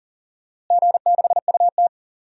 Morse code, heard as a keyed tone, about 40 wpm, 700 Hz. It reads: G6UT
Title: DXCC-100 Call Signs - 4 Characters - Repeated 2x - 40wpm